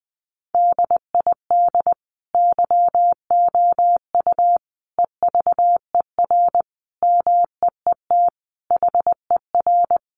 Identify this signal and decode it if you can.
Morse code, heard as a keyed tone, about 20 wpm, 700 Hz. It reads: DID YOU EVER MEET HER
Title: Sentences from Top 1100 Words 20wpm